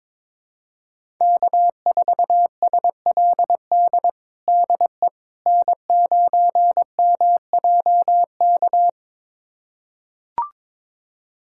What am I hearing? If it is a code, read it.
Morse code, heard as a keyed tone, about 22 wpm, 700 Hz. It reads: K4SLD DE N9MJK
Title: USA Callsign DE Callsign - 5 Characters - 22wpm